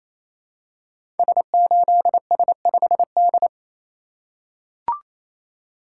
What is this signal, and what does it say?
Morse code, heard as a keyed tone, about 28 wpm, 700 Hz. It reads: S8S5D
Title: Code Groups - 5 Letter:Numbers - 28wpm